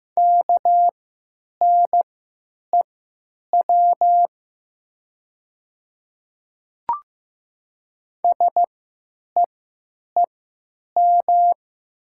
Morse code, heard as a keyed tone, about 15 wpm, spaced as for 9 wpm, 700 Hz. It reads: KNEW SEEM
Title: 3x Character Spacing - Top 500 Words - 15wpm